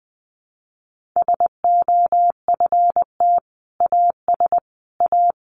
Morse code, heard as a keyed tone, about 20 wpm, 700 Hz. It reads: SOFT AS A
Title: Sentences from Top 1500 Words 20wpm